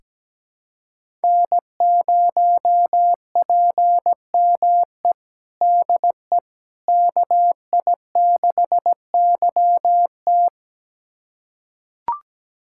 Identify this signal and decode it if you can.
Morse code, heard as a keyed tone, about 17 wpm, 700 Hz. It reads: N0PME DE KI6YT
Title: USA Callsign DE Callsign - 5 Characters - 17wpm